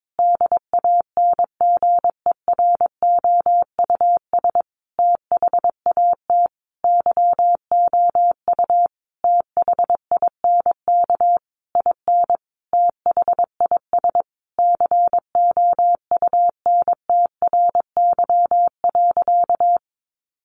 Morse code, heard as a keyed tone, about 22 wpm, 700 Hz. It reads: DANGEROUS THAT YOU THINK IN THIS COUNTRY.
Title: Sentences from Top 1200 Words 22wpm